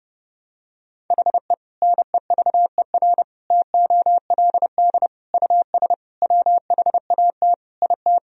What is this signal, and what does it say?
Morse code, heard as a keyed tone, about 30 wpm, 700 Hz. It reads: HE NEVER TOLD US WHAT IT
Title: Sentences from Top 400 Words and Abbreviations 30wpm